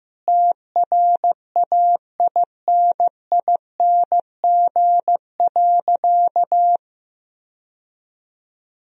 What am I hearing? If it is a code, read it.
Morse code, heard as a keyed tone, about 15 wpm, 700 Hz. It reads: TRAINING.
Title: Sentences from Top 1000 Words - Review Part 4 - 15wpm